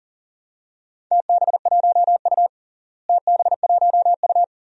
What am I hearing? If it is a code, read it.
Morse code, heard as a keyed tone, about 40 wpm, 700 Hz. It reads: TB1U TB1U
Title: DXCC-100 Call Signs - 4 Characters - Repeated 2x - 40wpm